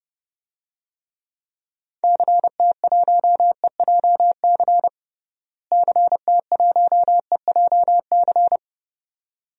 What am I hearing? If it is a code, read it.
Morse code, heard as a keyed tone, about 30 wpm, 700 Hz. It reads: CT1EJC CT1EJC
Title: DXCC-100 Call Signs - 6 Characters - Repeated 2x - 30wpm